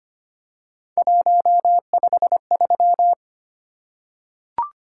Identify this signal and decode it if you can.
Morse code, heard as a keyed tone, about 25 wpm, 700 Hz. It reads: 153
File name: Three Letter-Number Combinations - 25wpm